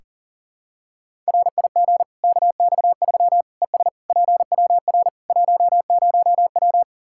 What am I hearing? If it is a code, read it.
Morse code, heard as a keyed tone, about 40 wpm, 700 Hz. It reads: RIG KX3 ES PWR 10W
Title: Practice QSO - Exchange 2 - Info - 40wpm